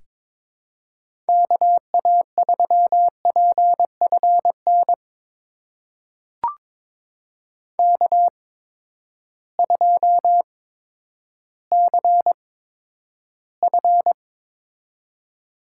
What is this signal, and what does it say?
Morse code, heard as a keyed tone, about 22 wpm, spaced as for 11 wpm, 700 Hz. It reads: KA3PFN K2CF
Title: USA Call Signs - 8x Character Spacing - 22wpm